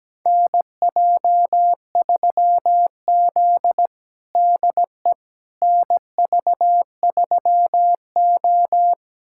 Morse code, heard as a keyed tone, about 17 wpm, 700 Hz. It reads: NJ3Z DE NV3O
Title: USA Callsign DE Callsign - 4 Characters - 17wpm